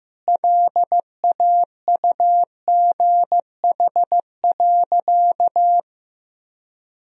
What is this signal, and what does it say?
Morse code, heard as a keyed tone, about 15 wpm, 700 Hz. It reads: LAUGH.
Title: Sentences from Top 1000 Words - Review Part 1 - 15wpm